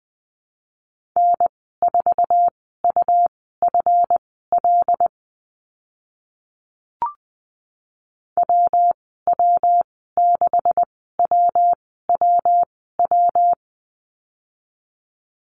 Morse code, heard as a keyed tone, about 20 wpm, spaced as for 14 wpm, 700 Hz. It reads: N4UFL WW6WWW
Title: 140 Lesson 23g - Letter f - Call Signs 20wpm